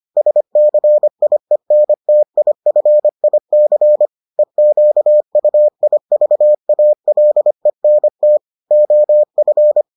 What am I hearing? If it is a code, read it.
Morse code, heard as a keyed tone, about 25 wpm, 600 Hz. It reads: SCIENTIFIC EQUIVALENT OF